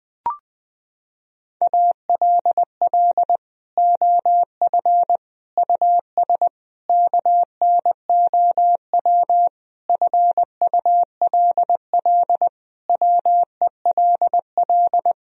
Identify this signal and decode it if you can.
Morse code, heard as a keyed tone, about 20 wpm, 700 Hz. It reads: ALL OF US KNOW FULL WELL
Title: Sentences from Top 1000 Words - Review Part 3 - 20wpm